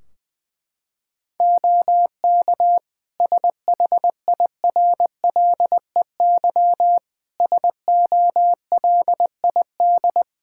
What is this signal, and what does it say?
Morse code, heard as a keyed tone, about 20 wpm, 700 Hz. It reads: OK SHIRLEY SOLID
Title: Practice QSO - Exchange 3 - More Info - 20wpm